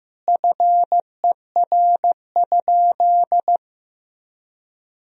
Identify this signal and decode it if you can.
Morse code, heard as a keyed tone, about 15 wpm, 700 Hz. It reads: FER?
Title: Sentences from Top 400 Words and Abbreviations 15wpm